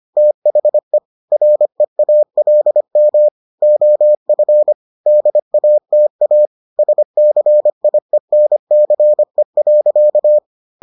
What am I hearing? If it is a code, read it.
Morse code, heard as a keyed tone, about 25 wpm, 600 Hz. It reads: THE REALM OF DATA SCIENCE.